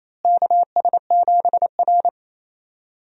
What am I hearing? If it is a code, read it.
Morse code, heard as a keyed tone, about 28 wpm, 700 Hz. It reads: KS7R
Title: USA Call Signs - 4 Characters - Repeated 3x - 28wpm